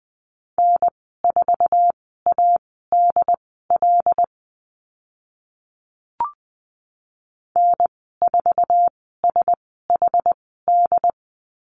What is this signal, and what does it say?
Morse code, heard as a keyed tone, about 20 wpm, spaced as for 15 wpm, 700 Hz. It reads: N4ADL N4SHD
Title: 076 Lesson 13g - Letter L - Call Signs 20wpm